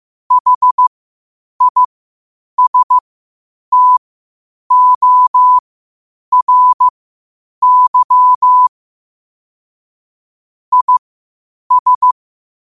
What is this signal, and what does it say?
Morse code, heard as a keyed tone, about 15 wpm, spaced as for 8 wpm, 1000 Hz. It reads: HISTORY IS